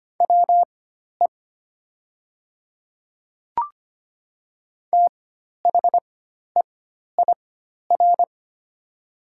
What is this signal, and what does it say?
Morse code, heard as a keyed tone, about 25 wpm, spaced as for 12 wpm, 700 Hz. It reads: WE THEIR
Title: 4x Character Spacing - Top 200 Words - 25wpm